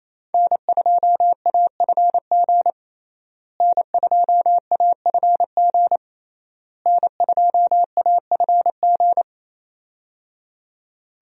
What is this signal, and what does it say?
Morse code, heard as a keyed tone, about 28 wpm, 700 Hz. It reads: N2AFG N2AFG N2AFG
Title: USA Call Signs - 5 Characters - Repeated 3x - 28wpm